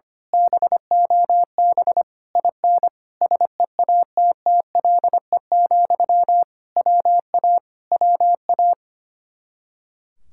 Morse code, heard as a keyed tone, about 24 wpm, 700 Hz. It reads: BOB IN SEATTLE, WA WA
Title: Speed-Racing Common QSO Phrases - 25wpm